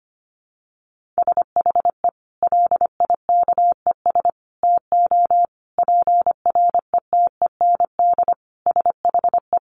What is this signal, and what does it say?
Morse code, heard as a keyed tone, about 25 wpm, 700 Hz. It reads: SHE LIKES TO PRETEND SHE
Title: Additional Easy Phrases 25wpm